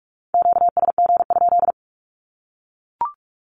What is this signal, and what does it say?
Morse code, heard as a keyed tone, about 44 wpm, 700 Hz. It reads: QSG?
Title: Q-Codes 45wpm